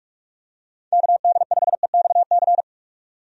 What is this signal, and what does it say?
Morse code, heard as a keyed tone, about 42 wpm, 700 Hz. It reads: KD5EXC
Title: USA Call Signs - 6 Characters - 45wpm